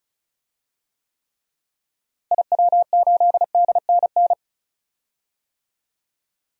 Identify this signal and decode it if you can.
Morse code, heard as a keyed tone, about 36 wpm, 700 Hz. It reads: IW8DNN
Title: DXCC-100 Call Signs - 6 Characters - Repeated 3x - 35wpm